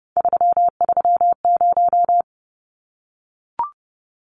Morse code, heard as a keyed tone, about 29 wpm, 700 Hz. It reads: 330
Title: Numbers - 3-Digit 30wpm